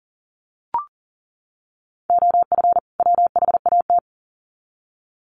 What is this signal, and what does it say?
Morse code, heard as a keyed tone, about 40 wpm, 700 Hz. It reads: OF WHAT
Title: Sets of 2 Words 40wpm